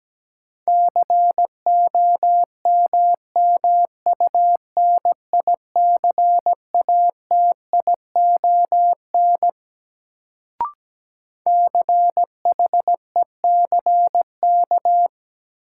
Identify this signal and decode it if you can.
Morse code, heard as a keyed tone, about 17 wpm, 700 Hz. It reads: COMMUNICATION CHECK
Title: Sentences from Top 1400 Words - New Word Drill Alt - 17wpm